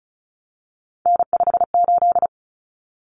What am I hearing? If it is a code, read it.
Morse code, heard as a keyed tone, about 34 wpm, 700 Hz. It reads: N58
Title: Common Cut Numbers - 3-Digit 35wpm